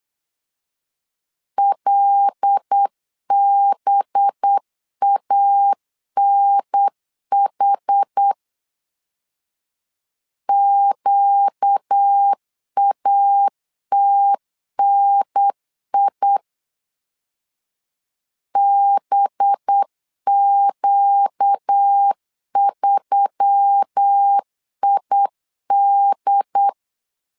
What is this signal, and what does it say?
Morse code, heard as a keyed tone, about 8 wpm, 800 Hz. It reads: LBANH QATNI BQ3ID
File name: LEZIONI / 5 LBVC / 5 LB 8wpm 2816